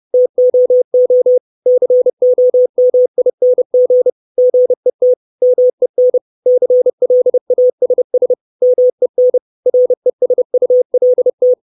Morse code, heard as a keyed tone, about 30 wpm, 500 Hz. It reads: TOO COMING GET MEN CLASS MEN RESULT
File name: 632 Words 30WPM